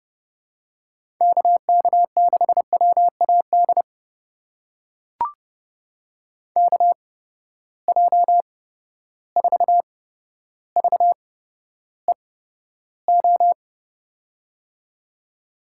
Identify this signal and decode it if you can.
Morse code, heard as a keyed tone, about 32 wpm, spaced as for 13 wpm, 700 Hz. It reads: KK6WAD KJ4VEO
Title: USA Call Signs - 6 Characters - 8x Character Spacing - 30wpm